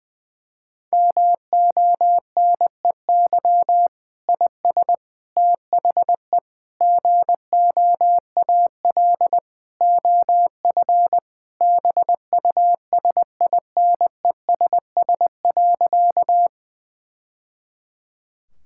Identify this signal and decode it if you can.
Morse code, heard as a keyed tone, about 20 wpm, 700 Hz. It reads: MONEY IS THE GOAL OF BUSINESS.
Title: Sentences from Top 1000 Words - Review Part 5 - 20wpm